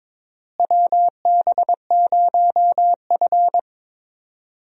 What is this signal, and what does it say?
Morse code, heard as a keyed tone, about 22 wpm, 700 Hz. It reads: WB0F
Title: USA Call Signs - 4 Characters - Repeated 2x - 22wpm